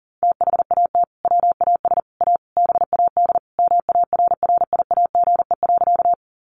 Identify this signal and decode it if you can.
Morse code, heard as a keyed tone, about 40 wpm, 700 Hz. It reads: THAT WAS A BAD MARRIAGE.
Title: Sentences from Top 800 Words 40wpm